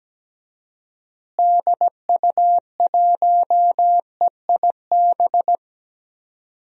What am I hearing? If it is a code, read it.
Morse code, heard as a keyed tone, about 17 wpm, 700 Hz. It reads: DU1EIB
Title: DXCC-100 Call Signs - 6 Characters - Repeated 3x - 17wpm